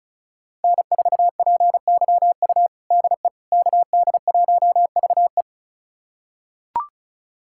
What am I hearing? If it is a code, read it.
Morse code, heard as a keyed tone, about 34 wpm, 700 Hz. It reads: N4PYU DE KD1VE
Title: USA Callsign DE Callsign - 5 Characters - 35wpm